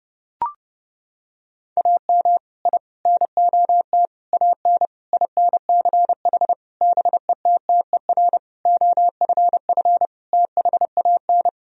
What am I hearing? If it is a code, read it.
Morse code, heard as a keyed tone, about 30 wpm, 700 Hz. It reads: AM I NOT AN INCH BETTER OFF THAN
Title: Sentences from Top 1000 Words - Review Part 3 - 30wpm